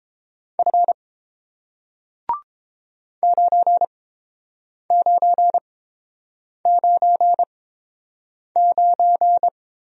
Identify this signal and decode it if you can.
Morse code, heard as a keyed tone, about 30 wpm, spaced as for 7 wpm, 700 Hz. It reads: F 9999